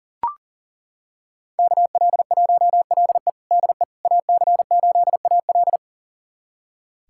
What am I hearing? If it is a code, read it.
Morse code, heard as a keyed tone, about 37 wpm, 700 Hz. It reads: KL1LE DE AC8AL
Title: USA Callsign DE Callsign - 5 Characters - 40wpm